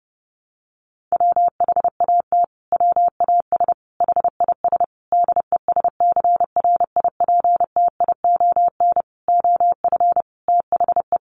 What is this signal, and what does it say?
Morse code, heard as a keyed tone, about 30 wpm, 700 Hz. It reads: WHAT WAS HIS DESCRIPTION OF THE
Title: Sentences from Top 1300 Words 30wpm